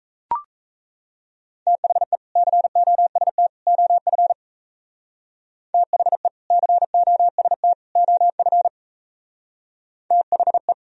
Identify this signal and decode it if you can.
Morse code, heard as a keyed tone, about 33 wpm, 700 Hz. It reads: THE COST OF THE COST OF THE
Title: Speed-Racing - Sets of 3 Words 28wpm